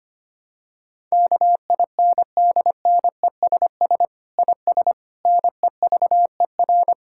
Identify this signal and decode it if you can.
Morse code, heard as a keyed tone, about 25 wpm, 700 Hz. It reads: KINDNESS IS NEVER
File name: practice / 25wpm / Difficult Sentences with Hints - 25wpm